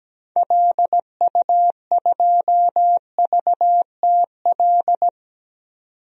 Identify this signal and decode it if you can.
Morse code, heard as a keyed tone, about 17 wpm, 700 Hz. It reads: LU2VTL
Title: DXCC-100 Call Signs - 6 Characters - 17wpm